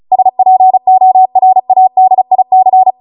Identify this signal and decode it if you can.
Morse code, heard as a keyed tone, about 36 wpm, 750 Hz. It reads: SPORADIC